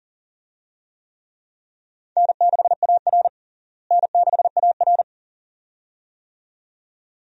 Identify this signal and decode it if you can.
Morse code, heard as a keyed tone, about 40 wpm, 700 Hz. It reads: N6AR N6AR
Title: USA Call Signs - 4 Characters - Repeated 2x - 40wpm